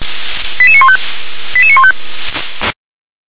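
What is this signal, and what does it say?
Five-tone selective call (ZVEI), heard as quick runs of five tones, tones of about 70 ms. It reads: 88915 88915
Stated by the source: ZVEI five-tone call received by SDR.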